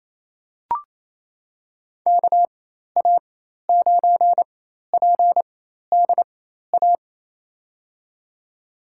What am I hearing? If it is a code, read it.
Morse code, heard as a keyed tone, about 28 wpm, spaced as for 13 wpm, 700 Hz. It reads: KA9PDA
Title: USA Call Signs - 4x Character Spacing - 28wpm